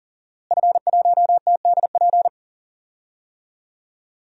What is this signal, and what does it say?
Morse code, heard as a keyed tone, about 40 wpm, 700 Hz. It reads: F1TDP
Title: DXCC-100 Call Signs - 5 Characters - Repeated 2x - 40wpm